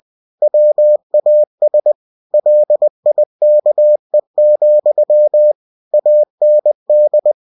Morse code, heard as a keyed tone, about 20 wpm, 600 Hz. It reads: WAS LIKE, AND
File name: cw-books / winnie-the-pooh / winnie-20wpm-page-0022